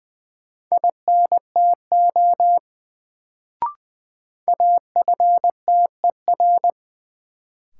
Morse code, heard as a keyed tone, about 20 wpm, 700 Hz. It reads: INTO AFTER
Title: Sets of 2 Words - New Word Drill - 20wpm